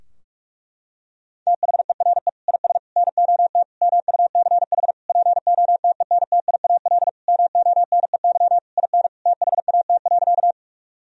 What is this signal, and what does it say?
Morse code, heard as a keyed tone, about 43 wpm, 700 Hz. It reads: THERE IS NOT MUCH POTENTIAL MONEY IN THAT.
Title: Sentences from Top 1000 Words - Review Part 4 - 45wpm